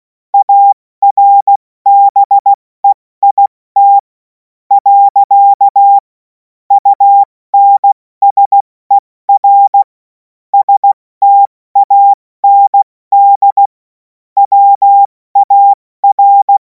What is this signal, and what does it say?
Morse code, heard as a keyed tone, about 16 wpm, 800 Hz. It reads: ARBEIT . UNSER STAND WAR
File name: Morsekurs / Witze / Witz 24. Februar 2026 70 F80